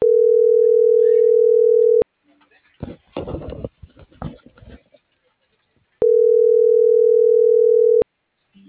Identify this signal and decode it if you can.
Ringback tone (North America), 440 + 480 Hz.